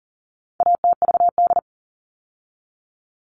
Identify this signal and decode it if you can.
Morse code, heard as a keyed tone, about 40 wpm, 700 Hz. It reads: ATVD
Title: All Cut Numbers - 4-Digit 40wpm